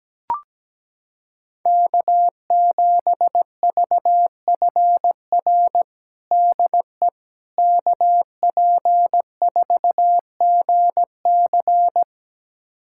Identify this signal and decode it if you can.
Morse code, heard as a keyed tone, about 17 wpm, 700 Hz. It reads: K7VFR DE KP4GC
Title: USA Callsign DE Callsign - 5 Characters - 17wpm